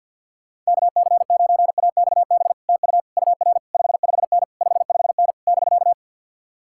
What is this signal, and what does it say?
Morse code, heard as a keyed tone, about 64 wpm, 700 Hz. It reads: KC9AXB TU UR 55N 55N <BK>